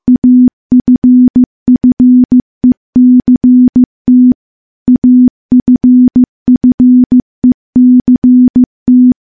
Morse code, heard as a keyed tone, about 15 wpm, 250 Hz. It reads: AFFECT AFFECT
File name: morse / morse-training-15WPM